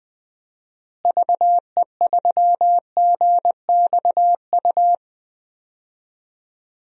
Morse code, heard as a keyed tone, about 20 wpm, 700 Hz. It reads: VE3GXU
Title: DXCC-100 Call Signs - 6 Characters - Repeated 2x - 20wpm